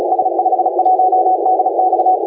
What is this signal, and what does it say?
Morse code, heard as a keyed tone, about 34 wpm, 740 Hz, with noise. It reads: S51DX
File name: la4c / sac-cw-2018 / CONTEST AUDIO 1600Z